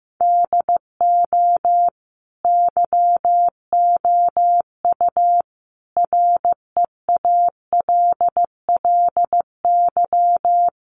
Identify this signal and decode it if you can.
Morse code, heard as a keyed tone, about 14 wpm, 700 Hz. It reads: DO YOU REALLY
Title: Sentences from Top 600 Words 15wpm